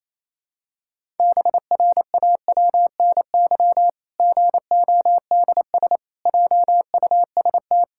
Morse code, heard as a keyed tone, about 28 wpm, 700 Hz. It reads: BRAWNY GODS JUST
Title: Pangram Challenge v2 - 28wpm